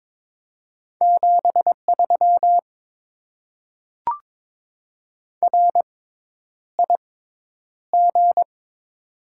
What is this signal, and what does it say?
Morse code, heard as a keyed tone, about 22 wpm, spaced as for 10 wpm, 700 Hz. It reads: 73 RIG
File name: Top 90 QSO Elements - 6x Character Spacing - 22wpm